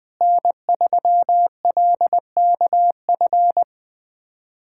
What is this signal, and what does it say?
Morse code, heard as a keyed tone, about 20 wpm, 700 Hz. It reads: N3LKF
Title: USA Callsign DE Callsign - 5 Characters - 20wpm